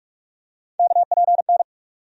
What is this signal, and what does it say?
Morse code, heard as a keyed tone, about 42 wpm, 700 Hz. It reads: KPN
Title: USA Call Sign Postfixes - 3 Letters 45wpm